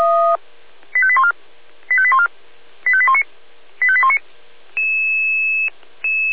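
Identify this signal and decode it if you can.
Five-tone selective call (ZVEI), heard as quick runs of five tones, tones of about 70 ms. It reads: 85814 85814 85819 85819
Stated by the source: Five-tone ZVEI selective call received by SDR.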